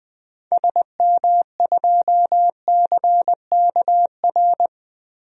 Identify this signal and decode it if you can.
Morse code, heard as a keyed tone, about 20 wpm, 700 Hz. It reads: SM2CKR
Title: DXCC-100 Call Signs - 6 Characters - 20wpm